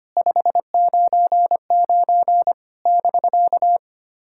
Morse code, heard as a keyed tone, about 25 wpm, 700 Hz. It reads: 599 <BK>
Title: SOTA QSO Practice 25wpm